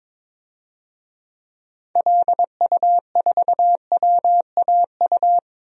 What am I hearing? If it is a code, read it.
Morse code, heard as a keyed tone, about 21 wpm, 700 Hz. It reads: LU4WAU
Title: DXCC-100 Call Signs - 6 Characters - Repeated 3x - 22wpm